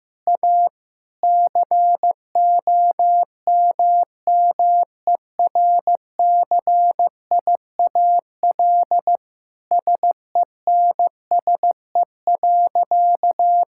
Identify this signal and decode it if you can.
Morse code, heard as a keyed tone, about 15 wpm, 700 Hz. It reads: A COMMERCIAL SENSE.
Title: Sentences from Top 1000 Words - Review Part 5 - 15wpm